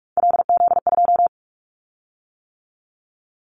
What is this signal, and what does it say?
Morse code, heard as a keyed tone, about 44 wpm, 700 Hz. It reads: LZ2
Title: Callsign Prefixes for Top 100 DXCC - 45wpm